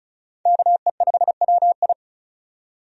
Morse code, heard as a keyed tone, about 36 wpm, 700 Hz. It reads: KE5WI
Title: USA Call Signs - 5 Characters - 35wpm